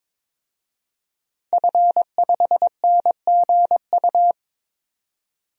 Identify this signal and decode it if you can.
Morse code, heard as a keyed tone, about 22 wpm, 700 Hz. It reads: F5NGU
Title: DXCC-100 Call Signs - 5 Characters - 22wpm